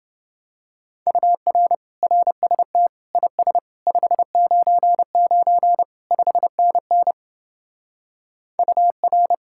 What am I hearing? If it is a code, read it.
Morse code, heard as a keyed tone, about 29 wpm, 700 Hz. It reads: UR RST IS 599 5NN UR